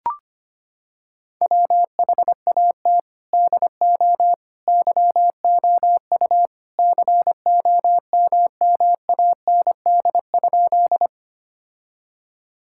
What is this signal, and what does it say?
Morse code, heard as a keyed tone, about 25 wpm, 700 Hz. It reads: WHAT DO YOU COMMAND?
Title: Sentences from Top 1600 Words 25wpm